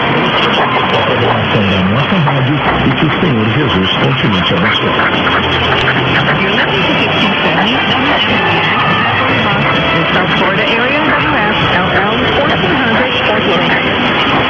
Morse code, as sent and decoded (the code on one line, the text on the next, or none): none